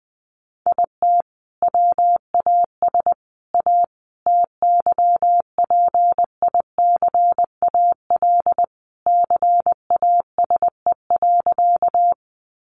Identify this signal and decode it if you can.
Morse code, heard as a keyed tone, about 20 wpm, 700 Hz. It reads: IT WAS A TYPICAL CASE.
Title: Sentences from Top 1100 Words 20wpm